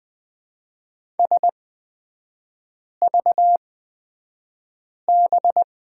Morse code, heard as a keyed tone, about 20 wpm, spaced as for 5 wpm, 700 Hz. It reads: SVB
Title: 250 Lesson 40e - BK - Three Characters 20wpm